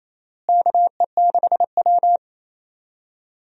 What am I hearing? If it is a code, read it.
Morse code, heard as a keyed tone, about 28 wpm, 700 Hz. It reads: KE6W
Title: USA Call Signs - 4 Characters - Repeated 2x - 28wpm